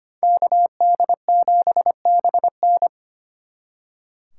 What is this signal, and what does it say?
Morse code, heard as a keyed tone, about 31 wpm, 700 Hz. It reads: KD7BN